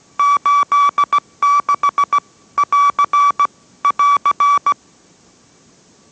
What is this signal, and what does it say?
Morse code, heard as a keyed tone, about 17 wpm, 1150 Hz. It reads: 86 <AR> <AR>